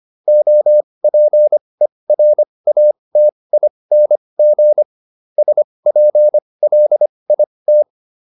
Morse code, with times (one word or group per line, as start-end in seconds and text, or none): 0.3-4.8 OPERATING
5.4-7.8 SPLIT